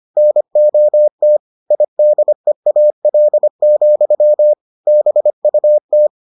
Morse code, heard as a keyed tone, about 25 wpm, 600 Hz. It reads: NOT IDEAL, BUT